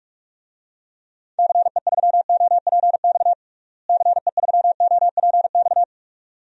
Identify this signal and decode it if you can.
Morse code, heard as a keyed tone, about 46 wpm, 700 Hz. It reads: CE3OPX CE3OPX
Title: DXCC-100 Call Signs - 6 Characters - Repeated 2x - 45wpm